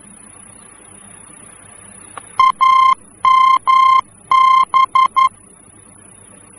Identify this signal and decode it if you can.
Morse code, heard as a keyed tone, about 11 wpm, 1050 Hz. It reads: AMB
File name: france / mx / AMB-F-VOR